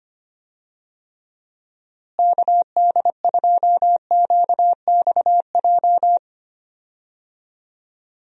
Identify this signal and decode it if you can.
Morse code, heard as a keyed tone, about 25 wpm, 700 Hz. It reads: KD2QXJ
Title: USA Call Signs - 6 Characters - Repeated 2x - 25wpm